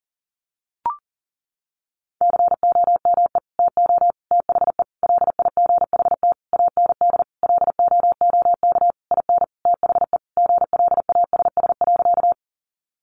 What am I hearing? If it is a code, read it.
Morse code, heard as a keyed tone, about 37 wpm, 700 Hz. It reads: COME TO THE LIGHT AND LOOK IN THE GLASS.
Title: Sentences from Top 1300 Words 40wpm